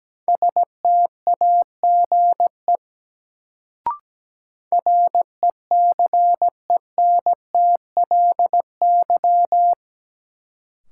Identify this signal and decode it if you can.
Morse code, heard as a keyed tone, about 17 wpm, 700 Hz. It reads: STAGE RECENTLY
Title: Sentences from Top 1200 Words - New Word Drill - 17wpm